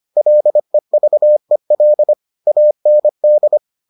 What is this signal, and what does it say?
Morse code, heard as a keyed tone, about 25 wpm, 600 Hz. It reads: LEVEL AND